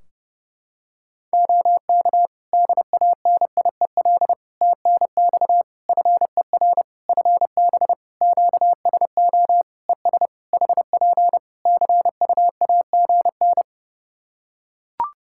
Morse code, heard as a keyed tone, about 30 wpm, 700 Hz. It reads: OK DANIEL TNX FER FB QSO ES HP CUAGN
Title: Practice QSO - Exchange 4 - Ending - 30wpm